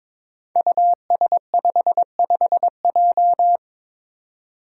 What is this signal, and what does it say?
Morse code, heard as a keyed tone, about 22 wpm, 700 Hz. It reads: US55J
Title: Five Challenging Letter-Number Combinations 22wpm